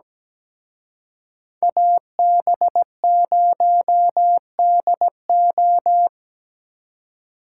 Morse code, heard as a keyed tone, about 20 wpm, 700 Hz. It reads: AB0DO
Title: Speed Racing - USA 2x2 Call Signs 17wpm